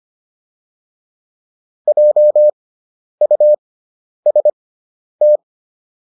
Morse code, heard as a keyed tone, about 25 wpm, spaced as for 10 wpm, 600 Hz. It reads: JUST